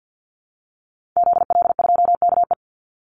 Morse code, heard as a keyed tone, about 50 wpm, 700 Hz. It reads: ZL2KE
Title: Call Signs Repeated 2x - 50wpm